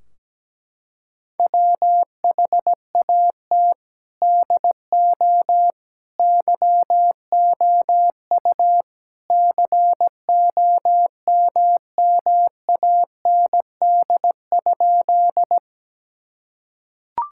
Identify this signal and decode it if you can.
Morse code, heard as a keyed tone, about 17 wpm, 700 Hz. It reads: WHAT DO YOU COMMAND?
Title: Sentences from Top 1600 Words 17wpm